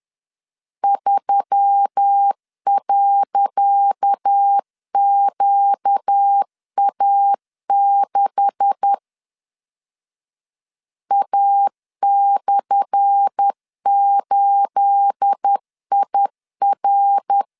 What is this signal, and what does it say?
Morse code, heard as a keyed tone, about 10 wpm, 800 Hz. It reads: 3.QA6 A/8IR
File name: LEZIONI / 10 SIGNS / 10C DOT 10wpm 2940